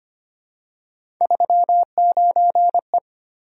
Morse code, heard as a keyed tone, about 25 wpm, 700 Hz. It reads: 39E
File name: Three Letter-Number Combinations - 25wpm